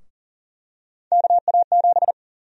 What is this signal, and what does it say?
Morse code, heard as a keyed tone, about 40 wpm, 700 Hz. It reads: KA7
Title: USA Call Sign Prefixes - 40wpm